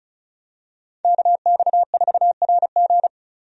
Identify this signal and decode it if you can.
Morse code, heard as a keyed tone, about 34 wpm, 700 Hz. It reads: KX4RG
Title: USA Call Signs - 5 Characters - Repeated 2x - 35wpm